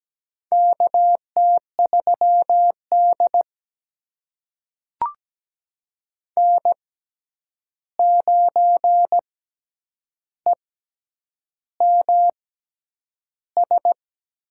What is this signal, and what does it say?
Morse code, heard as a keyed tone, about 17 wpm, spaced as for 8 wpm, 700 Hz. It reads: KT3D N9EMS
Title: USA Call Signs - 6x Character Spacing - 17wpm